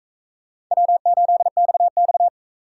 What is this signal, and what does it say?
Morse code, heard as a keyed tone, about 44 wpm, 700 Hz. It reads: W8XX